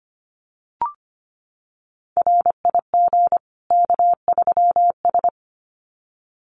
Morse code, heard as a keyed tone, about 25 wpm, 700 Hz. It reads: RIG K3S
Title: Common QSO Phrases 25wpm